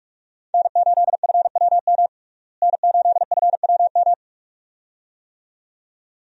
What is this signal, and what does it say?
Morse code, heard as a keyed tone, about 46 wpm, 700 Hz. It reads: N8FWM N8FWM
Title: USA Call Signs - 5 Characters - Repeated 2x - 45wpm